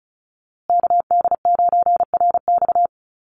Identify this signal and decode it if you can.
Morse code, heard as a keyed tone, about 34 wpm, 700 Hz. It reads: KD9RX
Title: USA Call Signs Repeated 3x 35wpm